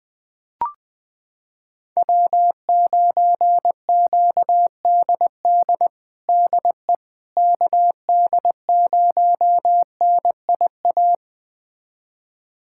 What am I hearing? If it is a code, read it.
Morse code, heard as a keyed tone, about 20 wpm, 700 Hz. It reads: W9QDD DE KD0NIA
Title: USA Callsign DE Callsign 20wpm